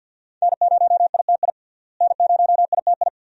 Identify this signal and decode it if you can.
Morse code, heard as a keyed tone, about 51 wpm, 700 Hz. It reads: N0ITI N0ITI
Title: USA Call Signs - 5 Characters - Repeated 2x - 50wpm